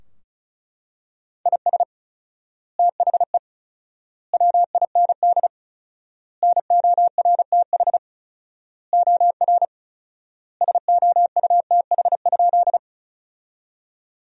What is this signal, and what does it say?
Morse code, heard as a keyed tone, about 36 wpm, 700 Hz. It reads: IS THE WIND NORTH OR SOUTH?
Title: 4x Word-Spacing - Sentences from Top 400 Words - 35wpm